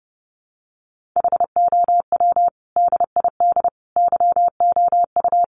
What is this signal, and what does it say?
Morse code, heard as a keyed tone, about 30 wpm, 700 Hz. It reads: HOW DID YOU
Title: Sentences from Top 1500 Words 30wpm